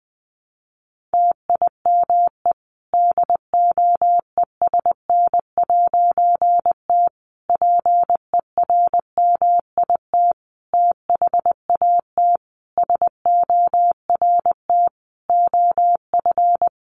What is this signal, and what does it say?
Morse code, heard as a keyed tone, about 20 wpm, 700 Hz. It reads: TIME DOESN'T PERMIT THAT SORT OF
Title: A Sound Of Thunder 20wpm